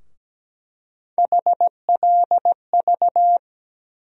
Morse code, heard as a keyed tone, about 17 wpm, 700 Hz. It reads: HLV
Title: USA Call Sign Postfixes - 3 Letters 17wpm